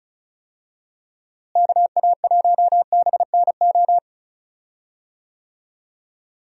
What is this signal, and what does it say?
Morse code, heard as a keyed tone, about 36 wpm, 700 Hz. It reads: KA1BNO
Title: USA Call Signs - 6 Characters - Repeated 3x - 35wpm